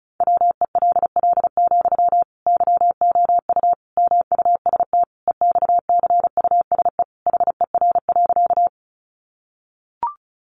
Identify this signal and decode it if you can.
Morse code, heard as a keyed tone, about 34 wpm, 700 Hz. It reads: WELL, YOU MUST EXCUSE HER.
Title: Sentences from Top 700 Words 35wpm